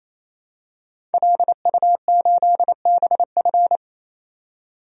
Morse code, heard as a keyed tone, about 28 wpm, 700 Hz. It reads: LU8BF
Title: DXCC-100 Call Signs - 5 Characters - 28wpm